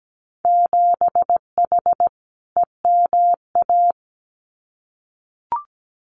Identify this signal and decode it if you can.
Morse code, heard as a keyed tone, about 17 wpm, 700 Hz. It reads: 7H EMA
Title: Winter Field Day Exchanges Encore 17wpm